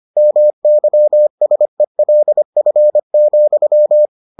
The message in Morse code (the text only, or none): MYSELF,